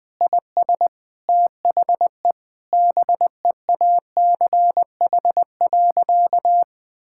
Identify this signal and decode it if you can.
Morse code, heard as a keyed tone, about 21 wpm, 700 Hz. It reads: IS THE BEACH.
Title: Sentences with 6-Word Phrases - Spoken Beginning - 20wpm